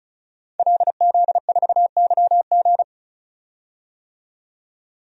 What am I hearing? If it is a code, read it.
Morse code, heard as a keyed tone, about 35 wpm, 700 Hz. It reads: LZ4YG
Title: DXCC-100 Call Signs - 5 Characters - 35wpm